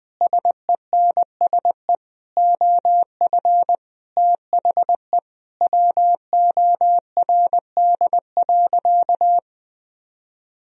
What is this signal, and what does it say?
Morse code, heard as a keyed tone, about 20 wpm, 700 Hz. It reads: SENSE OF THE WORD.
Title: Sentences from Top 1000 Words - Review Part 4 - 20wpm